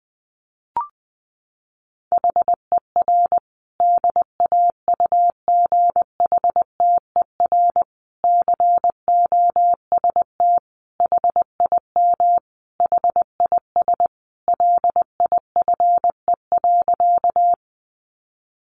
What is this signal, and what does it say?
Morse code, heard as a keyed tone, about 20 wpm, 700 Hz. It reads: HER DAUGHTER COST HIM HIS LIFE.
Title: Sentences from Top 1500 Words 20wpm